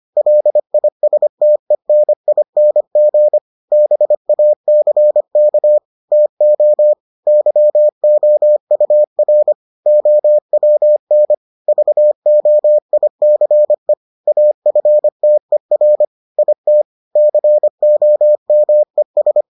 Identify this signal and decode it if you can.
Morse code, heard as a keyed tone, about 25 wpm, 600 Hz. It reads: LISTENING BACK TO YOUR OWN VOICE AFTER IT COMES